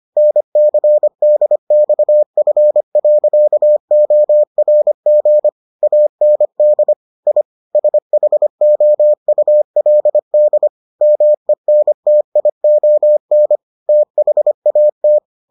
Morse code, heard as a keyed tone, about 25 wpm, 600 Hz. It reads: NCDXF.ORG AND I SHOULD MENTION THAT